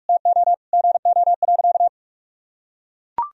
Morse code, heard as a keyed tone, about 42 wpm, 700 Hz. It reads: TO GO.
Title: Sentences from Top 1000 Words - Review Part 2 - 45wpm